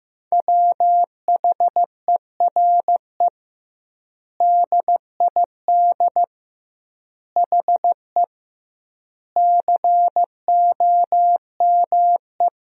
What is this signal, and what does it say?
Morse code, heard as a keyed tone, about 15 wpm, 700 Hz. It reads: WHERE DID HE COME
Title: 2x Word-Spacing - Sentences from Top 200 Words - 15wpm